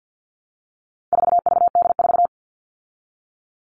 Morse code, heard as a keyed tone, about 48 wpm, 700 Hz. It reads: 4VD4
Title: All Cut Numbers - 4-Digit 50wpm